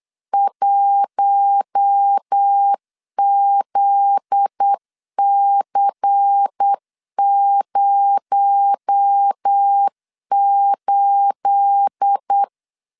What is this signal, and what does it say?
Morse code, heard as a keyed tone, about 8 wpm, 800 Hz. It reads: 1ZC08
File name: CW / DET / 1 9 1500C S B 8wpm 5624